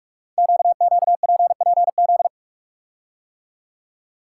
Morse code, heard as a keyed tone, about 44 wpm, 700 Hz. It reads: QQPPZ
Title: Code Groups - 5 Letters - 45wpm